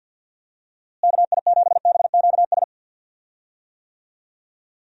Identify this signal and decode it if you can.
Morse code, heard as a keyed tone, about 52 wpm, 700 Hz. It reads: KI7BQS
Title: USA Call Signs - 6 Characters - Repeated 3x - 50wpm